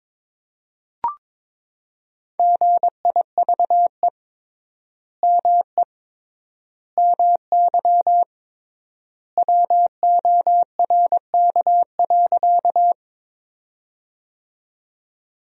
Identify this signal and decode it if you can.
Morse code, heard as a keyed tone, about 22 wpm, 700 Hz. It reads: GIVE ME MY WORK.
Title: 3x Word-Spacing - Sentences from Top 200 Words - 22wpm